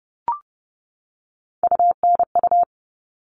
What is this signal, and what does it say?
Morse code, heard as a keyed tone, about 30 wpm, 700 Hz. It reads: UNU
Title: All Cut Numbers - 3-Digit 30wpm